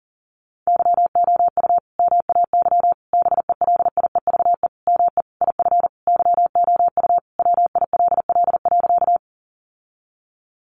Morse code, heard as a keyed tone, about 40 wpm, 700 Hz. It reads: YOU MAY BELIEVE ME IF YOU WILL.
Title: Sentences from Top 1000 Words 40wpm